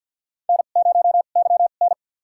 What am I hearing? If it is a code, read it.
Morse code, heard as a keyed tone, about 51 wpm, spaced as for 36 wpm, 700 Hz. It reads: N0YN
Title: USA Call Signs - 4 Characters - 2x Character Spacing - 50wpm